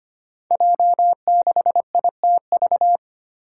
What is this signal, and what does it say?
Morse code, heard as a keyed tone, about 25 wpm, 700 Hz. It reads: J6ITV
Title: Five Challenging Letter-Number Combinations 25wpm